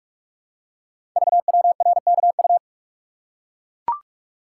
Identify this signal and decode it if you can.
Morse code, heard as a keyed tone, about 44 wpm, 700 Hz. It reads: VWRKU
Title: Code Groups - 5 Letter:Numbers - 45wpm